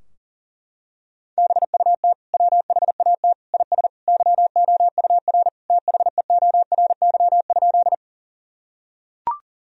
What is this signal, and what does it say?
Morse code, heard as a keyed tone, about 40 wpm, 700 Hz. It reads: BUT WHAT IS YOUR THEORY?
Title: Sentences from Top 1000 Words - Review Part 3 - 40wpm